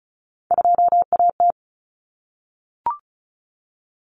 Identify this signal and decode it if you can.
Morse code, heard as a keyed tone, about 34 wpm, 700 Hz. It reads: 2AT
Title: Common Cut Numbers - 3-Digit 35wpm